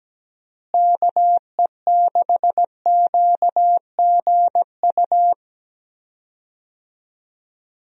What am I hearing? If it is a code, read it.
Morse code, heard as a keyed tone, about 17 wpm, 700 Hz. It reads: KE6QGU
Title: USA Call Signs - 6 Characters - Repeated 3x - 17wpm